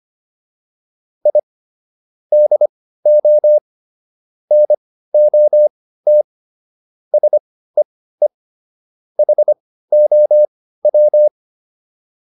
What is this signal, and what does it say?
Morse code, heard as a keyed tone, about 25 wpm, spaced as for 15 wpm, 600 Hz. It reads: I DO NOT SEE HOW